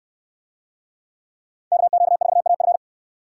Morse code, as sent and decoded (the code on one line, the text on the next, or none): -.- --.- ....- .- ..-.
KQ4AF